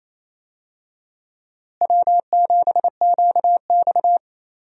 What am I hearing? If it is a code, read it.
Morse code, heard as a keyed tone, about 28 wpm, 700 Hz. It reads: W7QX
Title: USA Call Signs - 4 Characters - Repeated 2x - 28wpm